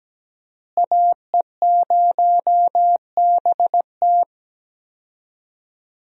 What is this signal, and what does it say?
Morse code, heard as a keyed tone, about 17 wpm, 700 Hz. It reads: AE0BT
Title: USA Call Signs - 5 Characters - Repeated 2x - 17wpm